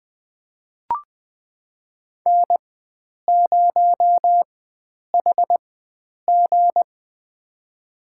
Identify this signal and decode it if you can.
Morse code, heard as a keyed tone, about 20 wpm, spaced as for 9 wpm, 700 Hz. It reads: N0HG
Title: USA Call Signs - 4x Character Spacing - 20wpm